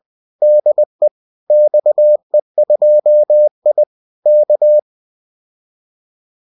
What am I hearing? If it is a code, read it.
Morse code, heard as a keyed tone, about 20 wpm, 600 Hz. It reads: DE XE2I K